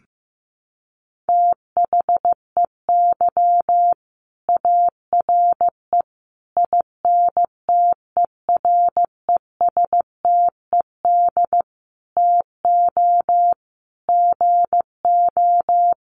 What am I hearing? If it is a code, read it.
Morse code, heard as a keyed tone, about 15 wpm, 700 Hz. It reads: THEY ARE INTERESTED TO GO
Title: Sentences from Top 900 Words 15wpm